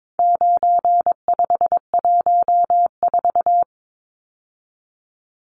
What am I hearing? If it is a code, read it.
Morse code, heard as a keyed tone, about 22 wpm, 700 Hz. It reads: 9514
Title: Numbers - 4-Digit 22wpm